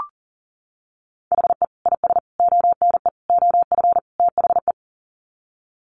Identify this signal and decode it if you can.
Morse code, heard as a keyed tone, about 40 wpm, 700 Hz. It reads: HE IS ONE OF THE
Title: Sets of 5 Words Encore 40wpm